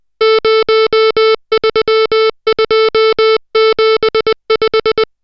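Morse code, heard as a keyed tone, about 20 wpm, 440 Hz, with noise. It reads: 03275
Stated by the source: Morse code drill of digits at 60 characters per minute.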